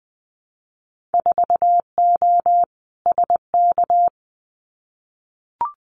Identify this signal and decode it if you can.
Morse code, heard as a keyed tone, about 20 wpm, 700 Hz. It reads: 4O SK
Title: Winter Field Day Exchanges Encore 20wpm